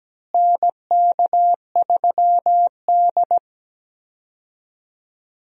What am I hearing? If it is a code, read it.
Morse code, heard as a keyed tone, about 17 wpm, 700 Hz. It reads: NK3D
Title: USA Call Signs - 4 Characters - Repeated 2x - 17wpm